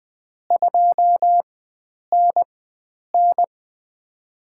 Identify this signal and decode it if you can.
Morse code, heard as a keyed tone, about 20 wpm, spaced as for 9 wpm, 700 Hz. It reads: 2NN
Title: USA Call Signs - 4x Character Spacing - 20wpm